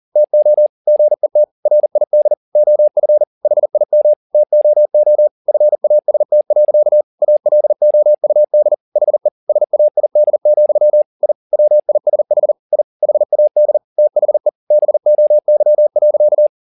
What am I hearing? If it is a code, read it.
Morse code, heard as a keyed tone, about 40 wpm, 600 Hz. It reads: TO GET RID OF HIM TOO FAST. ALOUD HE SAID, I WISH I HAD THE BOY.